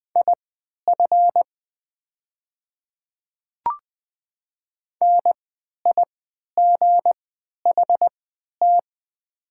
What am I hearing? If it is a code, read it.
Morse code, heard as a keyed tone, about 20 wpm, spaced as for 11 wpm, 700 Hz. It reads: IF NIGHT
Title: 3x Character Spacing - Top 300 Words - 20wpm